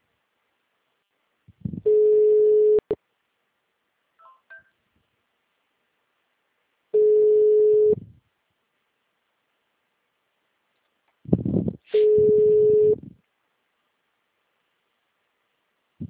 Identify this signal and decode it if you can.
Ringback tone (Europe, 425 Hz).
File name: out- -2002- - - .401